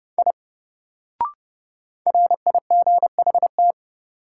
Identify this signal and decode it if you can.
Morse code, heard as a keyed tone, about 30 wpm, 700 Hz. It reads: I RIGHT
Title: Sentences from Top 200 Words - New Word Drill - 30wpm